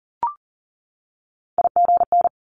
Morse code, heard as a keyed tone, about 38 wpm, 700 Hz. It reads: IGN
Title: Multi-letter Phonemes 40wpm